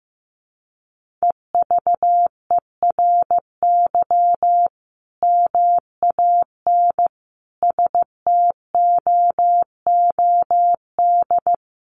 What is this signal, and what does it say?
Morse code, heard as a keyed tone, about 15 wpm, 700 Hz. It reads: EVERY MAN STOOD
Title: Sentences from Top 1400 Words 15wpm